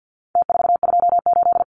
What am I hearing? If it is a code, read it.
Morse code, heard as a keyed tone, about 48 wpm, 700 Hz. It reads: T428
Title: Numbers - Contest Style - 4-Digit 50wpm